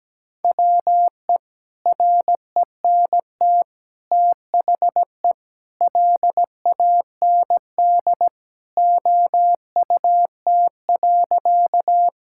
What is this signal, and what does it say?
Morse code, heard as a keyed tone, about 17 wpm, 700 Hz. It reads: WE RENT THE LAND OUT.
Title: Sentences from Top 1000 Words - Review Part 5 - 17wpm